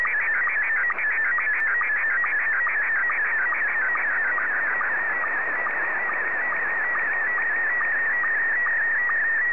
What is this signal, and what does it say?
SSTV (slow-scan television), Scottie 1.